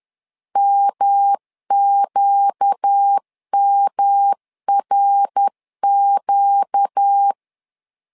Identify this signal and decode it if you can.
Morse code, heard as a keyed tone, about 10 wpm, 800 Hz. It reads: MQMRQ
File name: LEZIONI / 4 Q 10wpm 2210